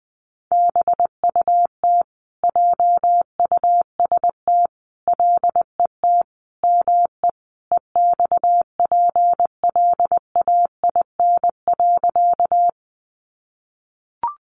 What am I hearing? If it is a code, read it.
Morse code, heard as a keyed tone, about 20 wpm, 700 Hz. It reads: BUT JUST LET ME EXPLAIN.
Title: Sentences from Top 1100 Words 20wpm